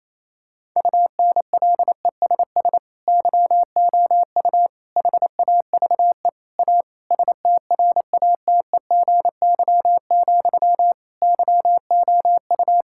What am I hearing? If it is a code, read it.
Morse code, heard as a keyed tone, about 28 wpm, 700 Hz. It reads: UNLESS YOU HAVE A STRATEGY, YOU
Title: Sentences from Top 1000 Words - Review Part 5 - 28wpm